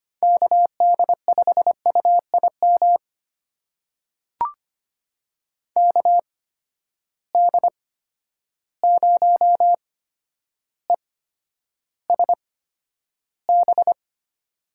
Morse code, heard as a keyed tone, about 25 wpm, spaced as for 11 wpm, 700 Hz. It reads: KD5UIM KD0ESB
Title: USA Call Signs - 6 Characters - 8x Character Spacing - 25wpm